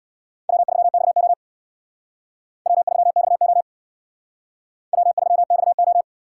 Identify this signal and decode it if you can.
Morse code, heard as a keyed tone, about 60 wpm, 690 Hz. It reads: W3XO W3XO W3XO
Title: Speed Racing - USA 1x2 Call Signs 50wpm